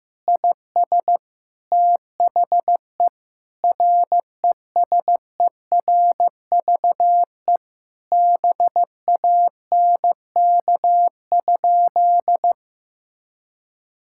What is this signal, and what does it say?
Morse code, heard as a keyed tone, about 16 wpm, 700 Hz. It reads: IS THE RESERVE BANK?
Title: Sentences from Top 1000 Words - Review Part 6 - 15wpm